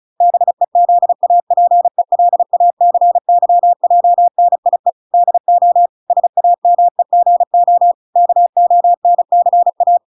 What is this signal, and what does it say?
Morse code, heard as a keyed tone, about 34 wpm, 700 Hz. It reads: BEZAPELACYJNIE DO SAMEGO KONCA